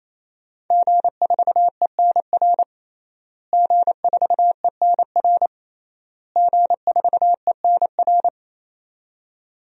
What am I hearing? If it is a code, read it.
Morse code, heard as a keyed tone, about 28 wpm, 700 Hz. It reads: G4ENR G4ENR G4ENR
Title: DXCC-100 Call Signs - 5 Characters - Repeated 3x - 28wpm